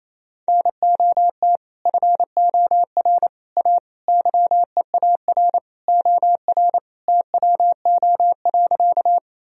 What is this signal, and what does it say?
Morse code, heard as a keyed tone, about 28 wpm, 700 Hz. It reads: NOT FOR A YEAR OR TWO.
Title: Sentences from Top 1000 Words - Review Part 1 - 28wpm